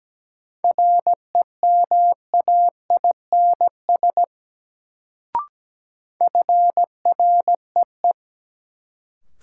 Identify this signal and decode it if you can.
Morse code, heard as a keyed tone, about 17 wpm, 700 Hz. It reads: REMAINS FREE
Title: Sets of 5 Words Encore - New Word Drill - 17wpm